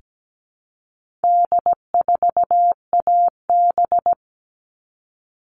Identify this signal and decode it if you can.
Morse code, heard as a keyed tone, about 17 wpm, 700 Hz. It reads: D4AB
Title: All Cut Numbers - 4-Digit 17wpm